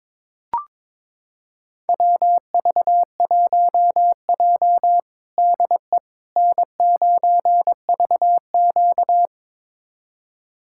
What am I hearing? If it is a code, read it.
Morse code, heard as a keyed tone, about 22 wpm, 700 Hz. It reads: WV1J DE N9VQ
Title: USA Callsign DE Callsign - 4 Characters - 22wpm